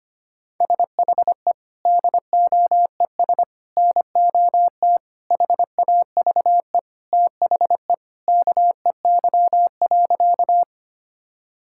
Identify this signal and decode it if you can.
Morse code, heard as a keyed tone, about 25 wpm, 700 Hz. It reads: SHE DOES NOT HAVE THE KEY.
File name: practice / 25wpm / Sentences with 3-Word Phrases - 25wpm